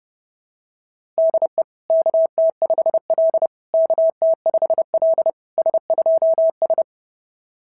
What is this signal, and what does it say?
Morse code, heard as a keyed tone, about 30 wpm, 660 Hz. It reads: DE KT5L KT5L S2S
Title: SOTA QSO Practice 30wpm